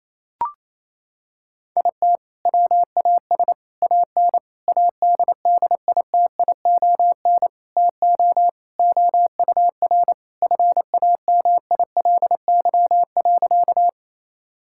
Morse code, heard as a keyed tone, about 28 wpm, 700 Hz. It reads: IT WAS AN ADDITION TO OUR FAMILY.
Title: Sentences from Top 1000 Words - Review Part 6 - 28wpm